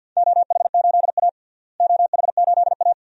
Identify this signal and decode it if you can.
Morse code, heard as a keyed tone, about 51 wpm, 700 Hz. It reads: OH8A OH8A
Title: DXCC-100 Call Signs - 4 Characters - Repeated 2x - 50wpm